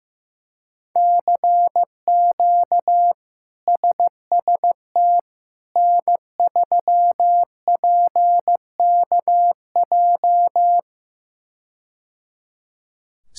Morse code, heard as a keyed tone, about 16 wpm, 700 Hz. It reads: CQ SST N3PKJ
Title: K1USN SST QSO Practice - 15wpm